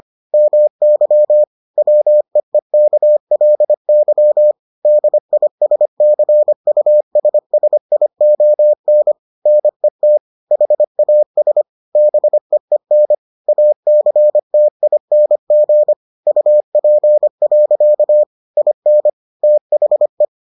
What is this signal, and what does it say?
Morse code, heard as a keyed tone, about 25 wpm, 600 Hz. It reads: MY WEEKLY DISCUSSION NET HAS BEEN ACTING UP. IN THE